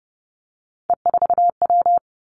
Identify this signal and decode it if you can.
Morse code, heard as a keyed tone, about 30 wpm, 700 Hz. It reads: E4W
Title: Three Challenging Letter-Number Combinations 30wpm